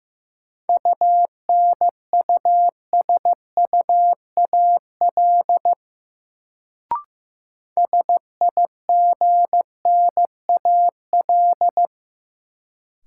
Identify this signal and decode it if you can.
Morse code, heard as a keyed tone, about 15 wpm, 700 Hz. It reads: UNUSUAL SIGNAL
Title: Sentences from Top 1400 Words - New Word Drill Alt - 15wpm